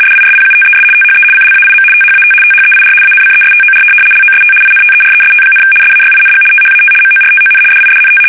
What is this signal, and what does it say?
RTTY, about 75 baud. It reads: DISPUTES THE CLAIM THAT THEIR ANCESTOR LOST INTEREST, SAYING THAT MORTON DIDNT
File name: fsk850-75B / teletype-corp